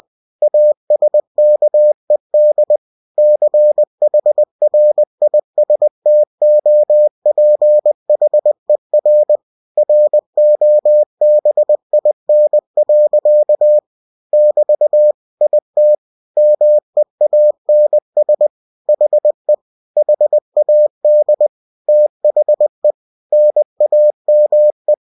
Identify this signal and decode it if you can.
Morse code, heard as a keyed tone, about 20 wpm, 600 Hz. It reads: ASKED CHRISTOPHER ROBIN. <BT> IT MEANS HE HAD THE NAME
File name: cw-books / winnie-the-pooh / winnie-20wpm-page-0002